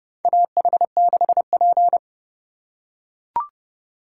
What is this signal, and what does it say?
Morse code, heard as a keyed tone, about 31 wpm, 700 Hz. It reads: AH6P
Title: USA Call Signs - 4 Characters - 6x Character Spacing - 30wpm